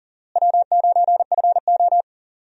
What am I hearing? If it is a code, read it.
Morse code, heard as a keyed tone, about 37 wpm, 700 Hz. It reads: W9FO
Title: USA Call Signs - 4 Characters - Repeated 3x - 40wpm